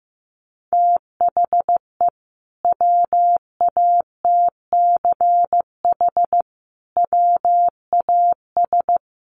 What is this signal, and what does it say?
Morse code, heard as a keyed tone, about 14 wpm, 700 Hz. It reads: THE WATCH WAS
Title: Sentences from Top 1300 Words 15wpm